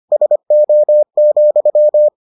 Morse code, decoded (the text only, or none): SO,